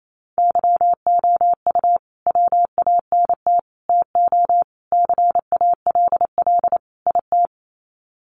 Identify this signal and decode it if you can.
Morse code, heard as a keyed tone, about 28 wpm, 700 Hz. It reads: YOU WANT TO CALL IT
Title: Sets of 5 Words Encore2 28wpm